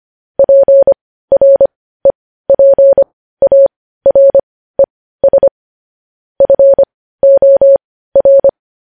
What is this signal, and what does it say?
Morse code, heard as a keyed tone, about 25 wpm, spaced as for 15 wpm, 550 Hz. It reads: PREPARES FOR
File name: podcast / News headlines-1x-25x15